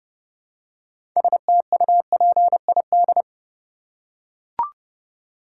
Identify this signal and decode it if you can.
Morse code, heard as a keyed tone, about 30 wpm, 700 Hz. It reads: STUPID
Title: Six Letter Words - 30wpm